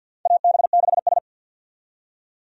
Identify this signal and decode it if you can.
Morse code, heard as a keyed tone, about 52 wpm, 700 Hz. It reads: AB6S
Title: USA Call Signs - 4 Characters - 50wpm